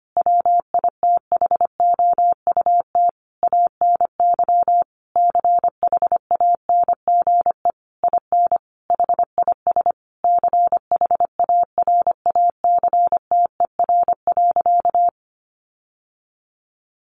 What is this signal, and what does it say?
Morse code, heard as a keyed tone, about 25 wpm, 700 Hz. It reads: WITHOUT ANY CHANGE IN HIS CHARACTER.
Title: Sentences from Top 1200 Words 25wpm